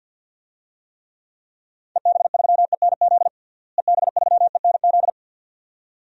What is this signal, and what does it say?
Morse code, heard as a keyed tone, about 50 wpm, 700 Hz. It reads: EB3ENZ EB3ENZ
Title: DXCC-100 Call Signs - 6 Characters - Repeated 2x - 50wpm